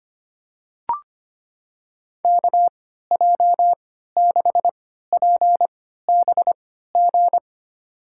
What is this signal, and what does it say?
Morse code, heard as a keyed tone, about 25 wpm, spaced as for 14 wpm, 700 Hz. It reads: KJ6PBG
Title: USA Call Signs - 6 Characters - 3x Character Spacing - 25wpm